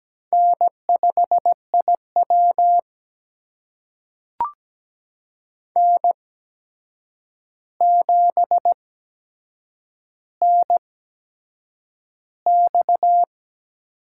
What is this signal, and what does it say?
Morse code, heard as a keyed tone, about 17 wpm, spaced as for 7 wpm, 700 Hz. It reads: N5IW N7NX
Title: USA Call Signs - 4 Characters - 8x Character Spacing - 17wpm